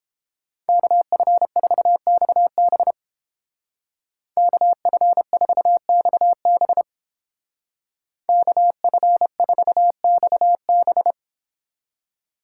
Morse code, heard as a keyed tone, about 29 wpm, 700 Hz. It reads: KF4XB KF4XB KF4XB